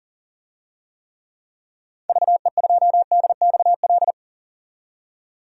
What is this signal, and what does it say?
Morse code, heard as a keyed tone, about 39 wpm, 700 Hz. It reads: VE2DXL
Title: DXCC-100 Call Signs - 6 Characters - 40wpm